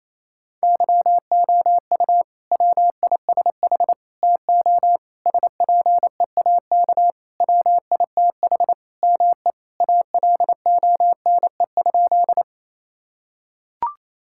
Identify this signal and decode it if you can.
Morse code, heard as a keyed tone, about 28 wpm, 700 Hz. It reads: YOU WISH TO SPEAK WITH ME ALONE?
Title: Sentences from Top 1600 Words 28wpm